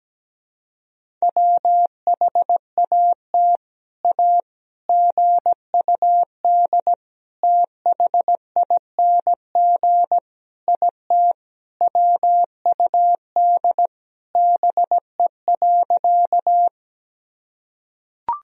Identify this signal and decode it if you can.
Morse code, heard as a keyed tone, about 17 wpm, 700 Hz. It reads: WHAT A GUD THING IT WUD BE.
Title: Sentences from Top 200 Words and Abbreviations 17wpm